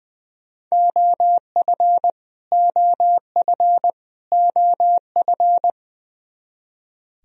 Morse code, heard as a keyed tone, about 20 wpm, 700 Hz. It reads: OF OF OF
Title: Top 100 Words Repeated 20wpm